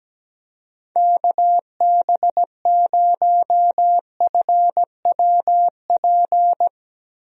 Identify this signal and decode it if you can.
Morse code, heard as a keyed tone, about 17 wpm, 700 Hz. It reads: KB0FWP
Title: USA Call Signs - 6 Characters - 17wpm